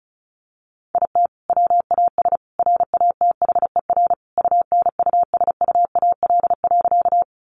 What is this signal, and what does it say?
Morse code, heard as a keyed tone, about 34 wpm, 700 Hz. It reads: IT WAS RATHER UNUSUAL.
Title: Sentences from Top 1400 Words 35wpm